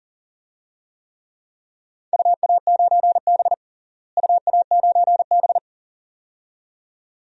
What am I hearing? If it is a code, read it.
Morse code, heard as a keyed tone, about 40 wpm, 700 Hz. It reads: UA9B UA9B
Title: DXCC-100 Call Signs - 4 Characters - Repeated 2x - 40wpm